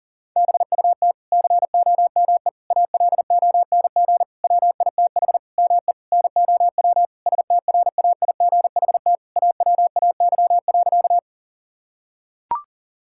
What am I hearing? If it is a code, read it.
Morse code, heard as a keyed tone, about 40 wpm, 700 Hz. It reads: BUT COME ALONG WITH ME NOW STRAIGHT AWAY.
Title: Sentences from Top 1000 Words - Review Part 6 - 40wpm